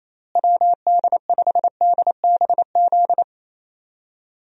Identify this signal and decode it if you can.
Morse code, heard as a keyed tone, about 28 wpm, 700 Hz. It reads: WD5DBZ
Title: USA Call Signs - 6 Characters - Repeated 2x - 28wpm